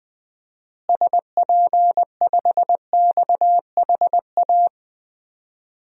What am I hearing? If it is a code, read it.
Morse code, heard as a keyed tone, about 21 wpm, 700 Hz. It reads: SP5XHA
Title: DXCC-100 Call Signs - 6 Characters - Repeated 2x - 20wpm